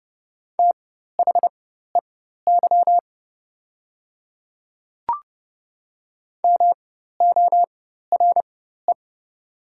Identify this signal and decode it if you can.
Morse code, heard as a keyed tone, about 30 wpm, spaced as for 14 wpm, 700 Hz. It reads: THEY MORE
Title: 4x Character Spacing - Top 100 Words - 30wpm